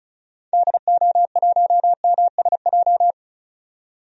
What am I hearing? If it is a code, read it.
Morse code, heard as a keyed tone, about 36 wpm, 690 Hz. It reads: DO1MSJ
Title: DXCC-100 Call Signs - 6 Characters - 35wpm